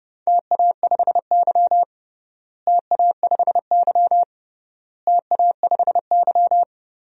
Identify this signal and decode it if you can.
Morse code, heard as a keyed tone, about 32 wpm, 700 Hz. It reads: TA5Y TA5Y TA5Y
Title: DXCC-100 Call Signs - 4 Characters - Repeated 3x - 30wpm